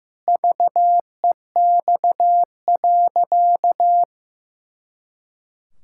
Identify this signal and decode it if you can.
Morse code, heard as a keyed tone, about 15 wpm, 700 Hz. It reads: VEX.
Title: Pangram Challenge v2 - 15wpm